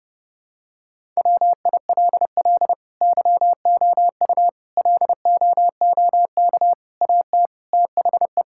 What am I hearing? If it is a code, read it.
Morse code, heard as a keyed tone, about 32 wpm, 700 Hz. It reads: WILL YOU LOOK AT THE
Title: Sentences from Top 1000 Words - Review Part 5 - 30wpm